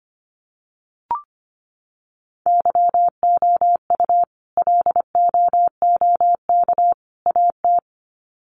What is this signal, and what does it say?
Morse code, heard as a keyed tone, about 25 wpm, 700 Hz. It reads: YOU LOOK AT
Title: Sets of 3 Words 25wpm